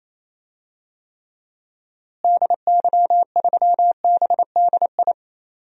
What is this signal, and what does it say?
Morse code, heard as a keyed tone, about 28 wpm, 700 Hz. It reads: DY3BDI
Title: DXCC-100 Call Signs - 6 Characters - Repeated 3x - 28wpm